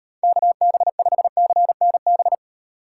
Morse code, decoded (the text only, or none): KB5CNB